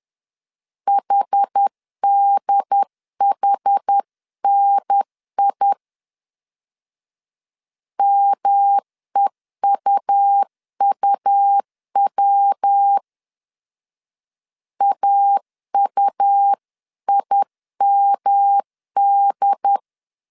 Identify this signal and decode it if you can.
Morse code, heard as a keyed tone, about 10 wpm, 800 Hz. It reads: HDHNI MEUUW AUIMD
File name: 10wpm